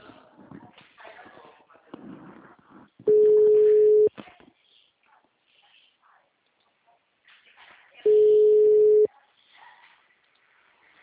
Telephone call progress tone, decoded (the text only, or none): ringback tone